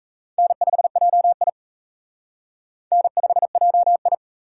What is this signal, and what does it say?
Morse code, heard as a keyed tone, about 35 wpm, 700 Hz. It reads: N5JI N5JI
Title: Speed Racing - USA Call Signs 28wpm